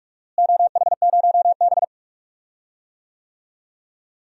Morse code, heard as a keyed tone, about 43 wpm, 700 Hz. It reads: OH0B
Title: DXCC-100 Call Signs - 4 Characters - Repeated 3x - 45wpm